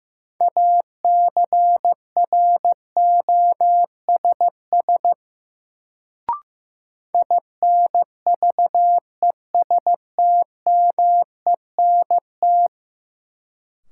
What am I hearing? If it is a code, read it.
Morse code, heard as a keyed tone, about 15 wpm, 700 Hz. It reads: ACROSS INVESTMENT
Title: Sentences from Top 700 Words - New Word Drill - 15wpm